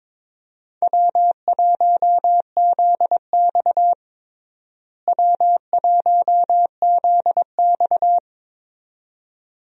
Morse code, heard as a keyed tone, about 22 wpm, 700 Hz. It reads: W1ZX W1ZX
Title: USA Call Signs - 4 Characters - Repeated 2x - 22wpm